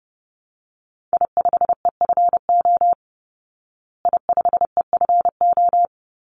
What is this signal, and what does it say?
Morse code, heard as a keyed tone, about 30 wpm, 700 Hz. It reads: I5EFO I5EFO
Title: Call Signs Repeated 3x - 30wpm